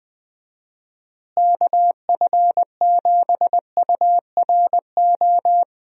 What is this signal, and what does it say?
Morse code, heard as a keyed tone, about 20 wpm, 700 Hz. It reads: KF7URO
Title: USA Call Signs - 6 Characters - Repeated 2x - 20wpm